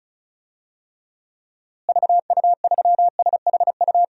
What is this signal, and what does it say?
Morse code, heard as a keyed tone, about 34 wpm, 700 Hz. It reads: VU3SHU
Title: DXCC-100 Call Signs - 6 Characters - Repeated 2x - 35wpm